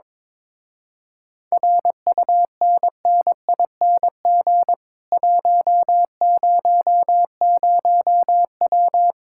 Morse code, heard as a keyed tone, about 22 wpm, 700 Hz. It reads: RUNNING 100W
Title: Speed-Racing Common QSO Phrases - 22wpm